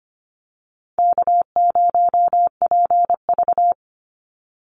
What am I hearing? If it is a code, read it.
Morse code, heard as a keyed tone, about 25 wpm, 700 Hz. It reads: K0PV
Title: Call Signs - 25wpm